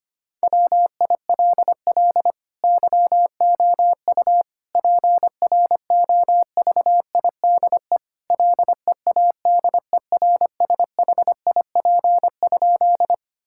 Morse code, heard as a keyed tone, about 25 wpm, 700 Hz. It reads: WILL YOU PROVIDE LEADERSHIP?
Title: Sentences from Top 1600 Words 25wpm